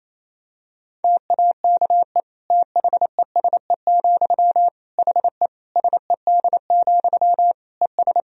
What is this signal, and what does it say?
Morse code, heard as a keyed tone, about 28 wpm, 700 Hz. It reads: TAKE THESE, HE SED, ES
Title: Sentences from Top 300 Words and Abbreviations 28wpm